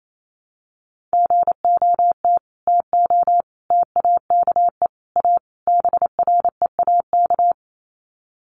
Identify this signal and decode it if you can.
Morse code, heard as a keyed tone, about 28 wpm, 700 Hz. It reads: GOT TO TAKE A BREAK
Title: Sets of 5 Words Encore 28wpm